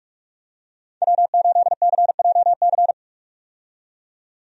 Morse code, heard as a keyed tone, about 46 wpm, 700 Hz. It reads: W8CJC
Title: USA Call Signs - 5 Characters - 45wpm